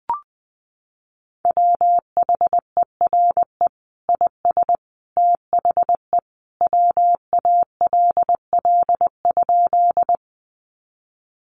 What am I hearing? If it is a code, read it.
Morse code, heard as a keyed tone, about 20 wpm, 700 Hz. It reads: WHERE IS THE WALL?
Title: Sentences from Top 1100 Words 20wpm